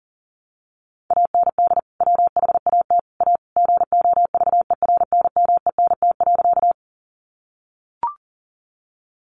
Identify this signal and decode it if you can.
Morse code, heard as a keyed tone, about 38 wpm, 700 Hz. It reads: AND WHAT A GOVERNMENT.
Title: Sentences from Top 600 Words 40wpm